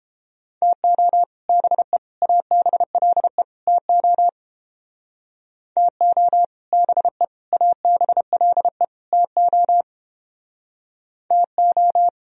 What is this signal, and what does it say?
Morse code, heard as a keyed tone, about 30 wpm, 700 Hz. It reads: TO BE ABLE TO TO BE ABLE TO TO